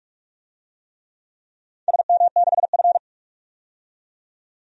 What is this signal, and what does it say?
Morse code, heard as a keyed tone, about 46 wpm, 700 Hz. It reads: SM6F
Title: DXCC-100 Call Signs - 4 Characters - 45wpm